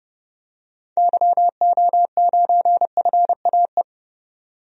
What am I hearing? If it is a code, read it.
Morse code, heard as a keyed tone, about 31 wpm, 700 Hz. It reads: YO9FAE
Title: DXCC-100 Call Signs - 6 Characters - 30wpm